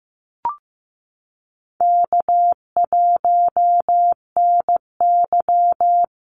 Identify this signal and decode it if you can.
Morse code, heard as a keyed tone, about 15 wpm, 700 Hz. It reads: K1NY
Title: Easy Call Signs Repeated 2x - 15wpm